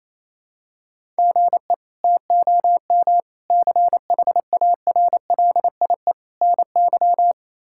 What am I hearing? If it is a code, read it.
Morse code, heard as a keyed tone, about 28 wpm, 700 Hz. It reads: GE TOM CHARLIE NY
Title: K1USN SST QSO Practice - 28wpm